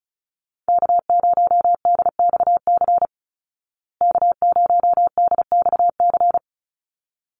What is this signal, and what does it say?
Morse code, heard as a keyed tone, about 34 wpm, 700 Hz. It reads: K0DXC K0DXC
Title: Call Signs Repeated 3x - 35wpm